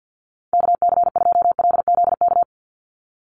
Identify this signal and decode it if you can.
Morse code, heard as a keyed tone, about 50 wpm, 700 Hz. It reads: KC2LZK
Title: USA Call Signs 50wpm